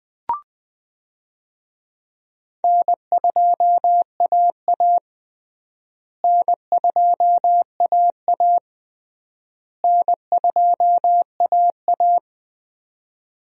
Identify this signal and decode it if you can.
Morse code, heard as a keyed tone, about 20 wpm, 700 Hz. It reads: N2AA N2AA N2AA
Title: USA Call Signs - 4 Characters - Repeated 3x - 20wpm